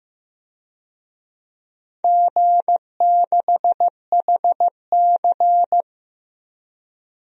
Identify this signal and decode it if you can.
Morse code, heard as a keyed tone, about 15 wpm, 700 Hz. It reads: G6HC
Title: DXCC-100 Call Signs - 4 Characters - 15wpm